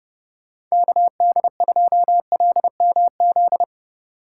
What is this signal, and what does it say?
Morse code, heard as a keyed tone, about 35 wpm, 700 Hz. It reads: KD2LMZ
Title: Speed Racing - USA Call Signs 30wpm